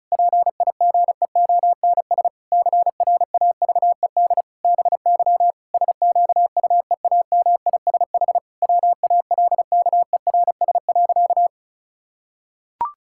Morse code, heard as a keyed tone, about 36 wpm, 700 Hz. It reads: PIGEONS CRAVED BY SQUEAMISH WALKERS.
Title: Pangram Challenge v2 - 35wpm